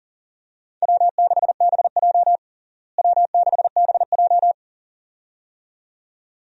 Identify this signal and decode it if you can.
Morse code, heard as a keyed tone, about 40 wpm, 700 Hz. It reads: W6BJ W6BJ
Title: USA Call Signs - 4 Characters - Repeated 2x - 40wpm